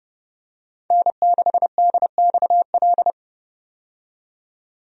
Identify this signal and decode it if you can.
Morse code, heard as a keyed tone, about 29 wpm, 700 Hz. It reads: N6DXL
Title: USA Call Signs - 5 Characters - 30wpm